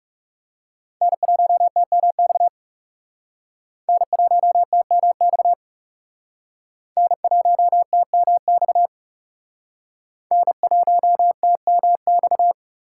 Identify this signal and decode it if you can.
Morse code, heard as a keyed tone, about 37 wpm, 700 Hz. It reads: N1TMX N1TMX N1TMX N1TMX
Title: Speed Racing - USA 1x3 Call Signs 30wpm